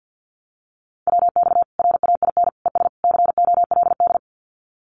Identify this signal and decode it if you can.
Morse code, heard as a keyed tone, about 50 wpm, 700 Hz. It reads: WX RAIN ES COLD
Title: Common QSO Phrases 50wpm